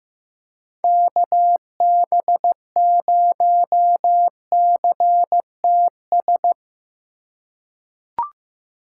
Morse code, heard as a keyed tone, about 16 wpm, 700 Hz. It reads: KB0CTS
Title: USA Call Signs - 4x Character Spacing - 15wpm